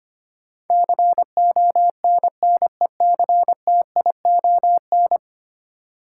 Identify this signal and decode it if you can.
Morse code, heard as a keyed tone, about 25 wpm, 690 Hz. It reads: CONNECTION
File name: 10 Letter Words - First 3 Letters Given - 25wpm